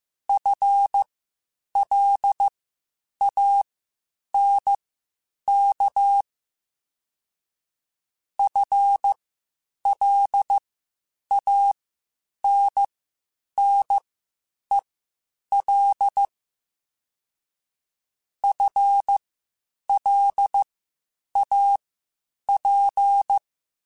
Morse code, heard as a keyed tone, about 15 wpm, spaced as for 8 wpm, 800 Hz. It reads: FLANK FLANNEL FLAP
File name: Spaced / FL Runs sp mp3s / FL Runs sp @13